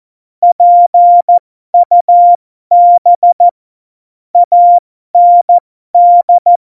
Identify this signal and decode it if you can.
Morse code, heard as a keyed tone, about 14 wpm, spaced as for 12 wpm, 700 Hz. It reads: PUB AND
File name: Morsekurs / Witze